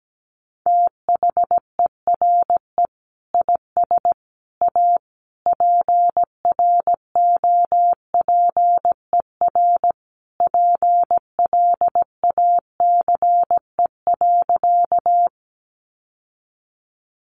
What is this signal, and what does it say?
Morse code, heard as a keyed tone, about 17 wpm, 700 Hz. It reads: THERE IS A PROPER PLACE.
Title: Sentences from Top 800 Words 17wpm